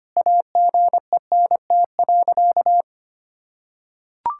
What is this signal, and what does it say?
Morse code, heard as a keyed tone, about 25 wpm, 700 Hz. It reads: AGENT.
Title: Sentences from Top 1600 Words 25wpm